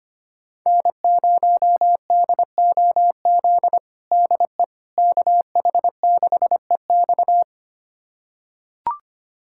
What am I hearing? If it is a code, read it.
Morse code, heard as a keyed tone, about 25 wpm, 700 Hz. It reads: N0DOZ DE KH6EX
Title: USA Callsign DE Callsign - 5 Characters - 25wpm